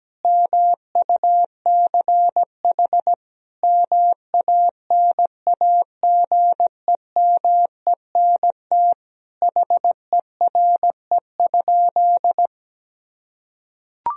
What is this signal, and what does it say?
Morse code, heard as a keyed tone, about 17 wpm, 700 Hz. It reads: MUCH MANAGEMENT HERE?
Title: Sentences from Top 1000 Words - Review Part 4 - 17wpm